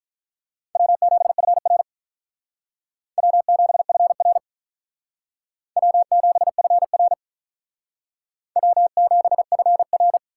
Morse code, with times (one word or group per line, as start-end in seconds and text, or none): 0.8-1.8 W7FR
3.2-4.4 W7FR
5.8-7.1 W7FR
8.6-10.2 W7FR